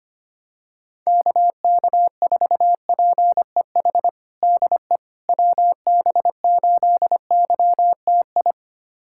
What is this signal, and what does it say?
Morse code, heard as a keyed tone, about 25 wpm, 700 Hz. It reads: KK4PEH DE WB8YTI
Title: USA Callsign DE Callsign - 6 Characters - 25wpm